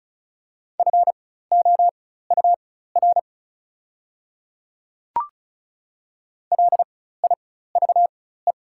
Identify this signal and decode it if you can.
Morse code, heard as a keyed tone, about 36 wpm, spaced as for 16 wpm, 690 Hz. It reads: FOUR LIVE
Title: 4x Character Spacing - Top 300 Words - 35wpm